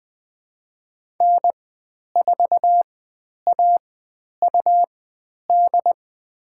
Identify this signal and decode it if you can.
Morse code, heard as a keyed tone, about 20 wpm, spaced as for 10 wpm, 700 Hz. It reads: N4AUD
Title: 109 Lesson 18g - Letter m - Call Signs 20wpm